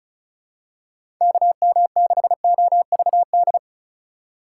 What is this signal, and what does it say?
Morse code, heard as a keyed tone, about 34 wpm, 700 Hz. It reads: KM6OVD
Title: USA Call Signs - 6 Characters - 2x Character Spacing - 35wpm